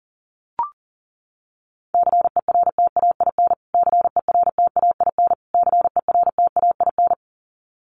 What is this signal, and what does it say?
Morse code, heard as a keyed tone, about 40 wpm, 700 Hz. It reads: CERTAIN CERTAIN CERTAIN
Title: Top 500 Words Repeated 40wpm